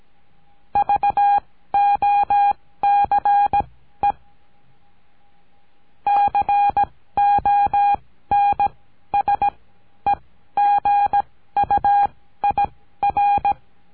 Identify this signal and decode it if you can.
Morse code, heard as a keyed tone, about 17 wpm, spaced as for 13 wpm, 810 Hz. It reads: VOCE CONSEGUIR
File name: CURSO DE TELEGRAFIA